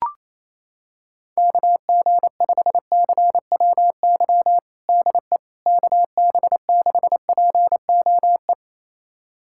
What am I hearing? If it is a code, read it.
Morse code, heard as a keyed tone, about 27 wpm, 700 Hz. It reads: KG5CWY DE KB6POE
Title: USA Callsign DE Callsign - 6 Characters - 28wpm